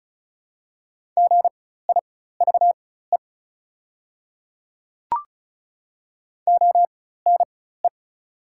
Morse code, heard as a keyed tone, about 34 wpm, spaced as for 16 wpm, 690 Hz. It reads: GIVE ONE
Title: 4x Character Spacing - Top 500 Words - 35wpm